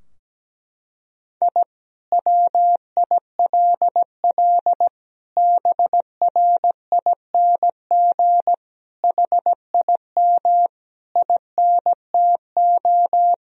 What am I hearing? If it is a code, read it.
Morse code, heard as a keyed tone, about 17 wpm, 700 Hz. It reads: I WILL BRING HIM INTO
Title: Sentences from Top 1000 Words - Review Part 3 - 17wpm